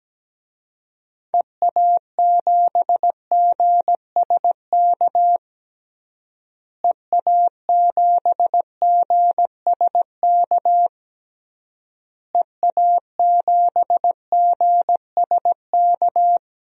Morse code, heard as a keyed tone, about 17 wpm, 700 Hz. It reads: EA7GSK EA7GSK EA7GSK
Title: DXCC-100 Call Signs - 6 Characters - Repeated 3x - 17wpm